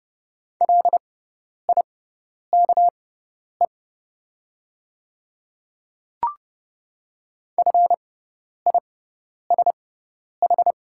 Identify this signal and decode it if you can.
Morse code, heard as a keyed tone, about 30 wpm, spaced as for 10 wpm, 690 Hz. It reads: LIKE FISH
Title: 6x Character Spacing - Top 300 Words - 30wpm